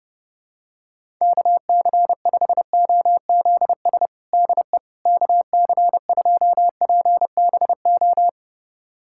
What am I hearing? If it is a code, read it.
Morse code, heard as a keyed tone, about 30 wpm, 700 Hz. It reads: KC5OZS DE KC2PBO
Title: USA Callsign DE Callsign 30wpm